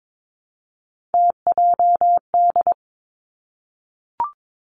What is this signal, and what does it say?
Morse code, heard as a keyed tone, about 22 wpm, 700 Hz. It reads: TJD
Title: Three Challenging Letter-Number Combinations 22wpm